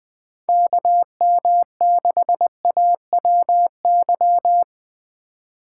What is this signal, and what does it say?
Morse code, heard as a keyed tone, about 20 wpm, 700 Hz. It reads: KM6AWY
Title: USA Call Signs - 6 Characters - 3x Character Spacing - 20wpm